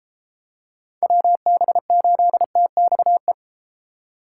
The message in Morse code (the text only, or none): WB8TXE